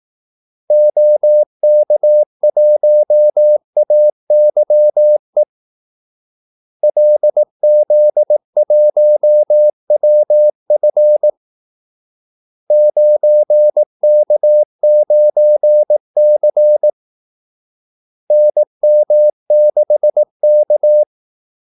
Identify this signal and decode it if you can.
Morse code, heard as a keyed tone, about 18 wpm, 600 Hz. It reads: OK1AYE LZ1WF 9K9C NM6K
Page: Kaldesignaler 18 wpm